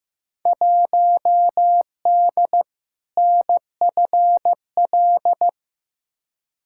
Morse code, heard as a keyed tone, about 15 wpm, 700 Hz. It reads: 1D NFL
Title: Field Day Exchanges - 15wpm